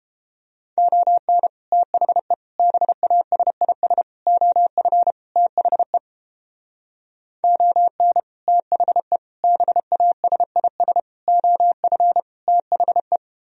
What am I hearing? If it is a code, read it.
Morse code, heard as a keyed tone, about 31 wpm, 700 Hz. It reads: ON THE BASIS OF THE ON THE BASIS OF THE